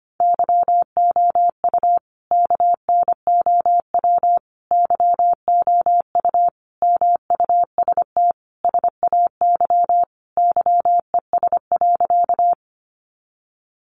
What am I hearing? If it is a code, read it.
Morse code, heard as a keyed tone, about 25 wpm, 700 Hz. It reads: YOU KNOW YOU MUST SAY YES.
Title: Sentences from Top 500 Words 25wpm